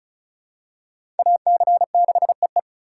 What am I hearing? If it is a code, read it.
Morse code, heard as a keyed tone, about 36 wpm, 700 Hz. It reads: AC6EE
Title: USA Call Signs - 5 Characters - 4x Character Spacing - 35wpm